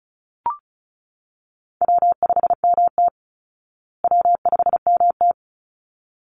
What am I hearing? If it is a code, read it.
Morse code, heard as a keyed tone, about 34 wpm, 700 Hz. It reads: W5MT W5MT
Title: Easy Call Signs Repeated 2x - 35wpm